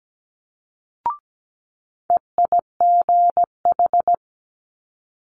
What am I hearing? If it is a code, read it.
Morse code, heard as a keyed tone, about 17 wpm, 700 Hz. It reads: EIGH
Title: Multi-letter Phonemes Quick Fire 17wpm